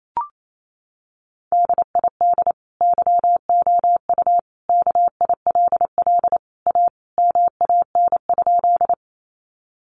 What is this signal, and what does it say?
Morse code, heard as a keyed tone, about 28 wpm, 700 Hz. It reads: DID YOU KILL A MAN?
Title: Sentences from Top 1400 Words 28wpm